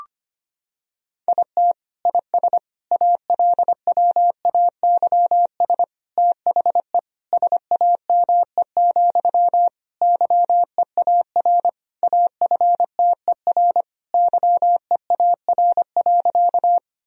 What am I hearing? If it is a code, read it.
Morse code, heard as a keyed tone, about 25 wpm, 700 Hz. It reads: IT IS ALWAYS THE SAME, YEAR AFTER YEAR.
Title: Sentences from Top 1000 Words - Review Part 1 - 25wpm